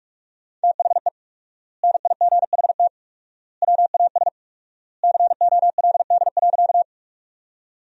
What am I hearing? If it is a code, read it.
Morse code, heard as a keyed tone, about 45 wpm, 700 Hz. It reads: THE NIGHT WAS COLD.
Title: 4x Word-Spacing - Sentences from Top 400 Words - 45wpm